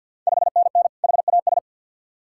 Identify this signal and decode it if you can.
Morse code, heard as a keyed tone, about 51 wpm, 690 Hz. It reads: 5NN HAS
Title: Ontario QSO Party Exchange 50wpm